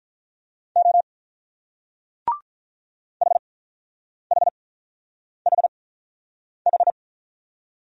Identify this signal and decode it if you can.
Morse code, heard as a keyed tone, about 44 wpm, spaced as for 9 wpm, 700 Hz. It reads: O HHHH